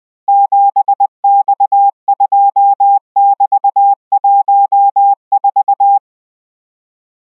Morse code, heard as a keyed tone, about 20 wpm, 800 Hz. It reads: 7X2<BT>14